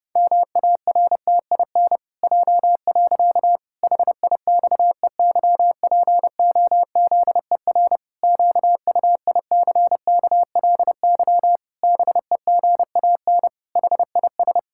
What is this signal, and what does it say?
Morse code, heard as a keyed tone, about 32 wpm, 700 Hz. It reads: MARTIN J. HIXEYPOZER QUICKLY BEGAN HIS
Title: Pangram Challenge v2 - 30wpm